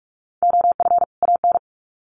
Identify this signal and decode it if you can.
Morse code, heard as a keyed tone, about 44 wpm, 700 Hz. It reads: OF AN
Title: Sets of 2 Words 45wpm